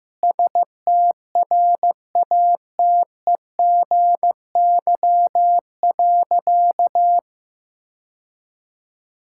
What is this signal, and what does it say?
Morse code, heard as a keyed tone, about 14 wpm, 700 Hz. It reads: STRATEGY.
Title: Sentences from Top 1000 Words - Review Part 5 - 15wpm